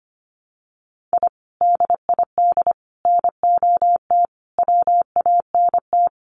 Morse code, heard as a keyed tone, about 25 wpm, 700 Hz. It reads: I DID NOT WANT
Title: Sets of 5 Words Encore 25wpm